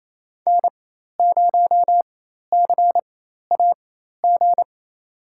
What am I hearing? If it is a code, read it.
Morse code, heard as a keyed tone, about 27 wpm, spaced as for 13 wpm, 700 Hz. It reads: N0CAG
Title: USA Call Signs - 5 Characters - 4x Character Spacing - 28wpm